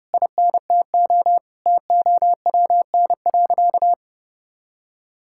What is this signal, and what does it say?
Morse code, heard as a keyed tone, about 30 wpm, 700 Hz. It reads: INTO TOWN.
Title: Sentences from Top 1000 Words - Review Part 6 - 30wpm